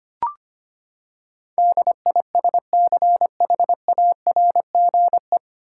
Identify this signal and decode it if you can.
Morse code, heard as a keyed tone, about 25 wpm, 700 Hz. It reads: DISCHARGE
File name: Nine Letter Words - 25wpm